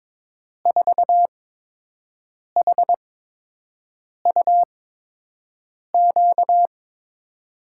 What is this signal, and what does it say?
Morse code, heard as a keyed tone, about 23 wpm, spaced as for 6 wpm, 700 Hz. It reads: 4HUQ
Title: USA Call Signs - 8x Character Spacing - 22wpm